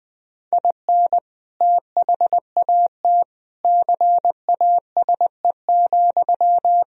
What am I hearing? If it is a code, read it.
Morse code, heard as a keyed tone, about 21 wpm, 700 Hz. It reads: IN THAT CASE,
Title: Sentences from Top 1000 Words - Review Part 2 - 20wpm